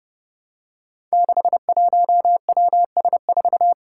Morse code, heard as a keyed tone, about 30 wpm, 700 Hz. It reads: 61WS4
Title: Five Challenging Letter-Number Combinations 30wpm